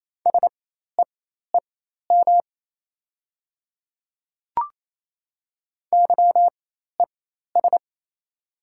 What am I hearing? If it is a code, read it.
Morse code, heard as a keyed tone, about 28 wpm, spaced as for 13 wpm, 700 Hz. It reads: SEEM YES
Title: 4x Character Spacing - Top 500 Words - 28wpm